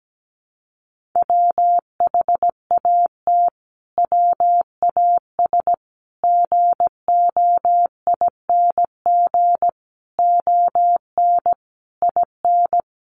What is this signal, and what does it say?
Morse code, heard as a keyed tone, about 17 wpm, 700 Hz. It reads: WHAT WAS GOING ON IN
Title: Sets of 5 Words Encore 17wpm